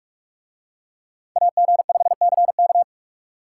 Morse code, heard as a keyed tone, about 41 wpm, 700 Hz. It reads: AG5CK